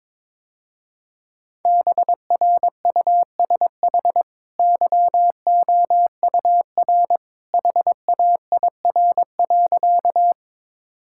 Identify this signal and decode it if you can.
Morse code, heard as a keyed tone, about 22 wpm, 700 Hz. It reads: BRUSH YOUR HAIR.
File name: practice / 22wpm / Easy Sentences with Hints - 22wpm